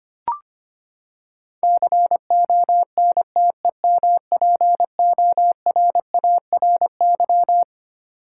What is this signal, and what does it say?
Morse code, heard as a keyed tone, about 25 wpm, 700 Hz. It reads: CONTEMPORARY
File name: Twelve Letter Words - 25wpm